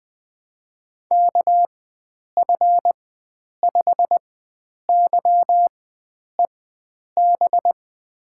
Morse code, heard as a keyed tone, about 20 wpm, spaced as for 9 wpm, 700 Hz. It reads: KF5YEB
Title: USA Call Signs - 6 Characters - 4x Character Spacing - 20wpm